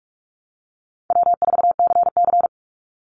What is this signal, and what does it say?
Morse code, heard as a keyed tone, about 44 wpm, 700 Hz. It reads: W4CC
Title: Call Signs - 45wpm